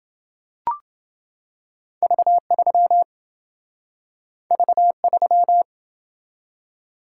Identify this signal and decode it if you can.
Morse code, heard as a keyed tone, about 27 wpm, 700 Hz. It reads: V3 V3